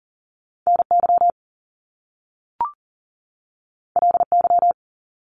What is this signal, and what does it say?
Morse code, heard as a keyed tone, about 42 wpm, 700 Hz. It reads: NY LY
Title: Top Two Letter Combinations in Words 40wpm